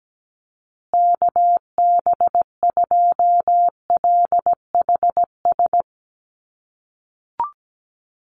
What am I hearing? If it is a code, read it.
Morse code, heard as a keyed tone, about 17 wpm, 700 Hz. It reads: KB2LHS
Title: USA Call Signs Repeated 3x 17wpm